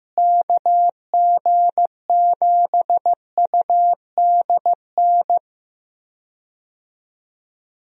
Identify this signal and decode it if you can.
Morse code, heard as a keyed tone, about 15 wpm, 700 Hz. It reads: KG7UDN
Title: USA Call Signs - 6 Characters - Repeated 3x - 15wpm